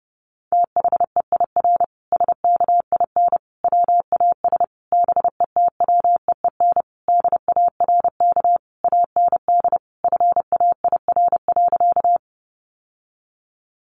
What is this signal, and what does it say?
Morse code, heard as a keyed tone, about 30 wpm, 700 Hz. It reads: THEIR SKIN WAS BETWEEN DARK AND FAIR.
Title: Sentences from Top 1300 Words 30wpm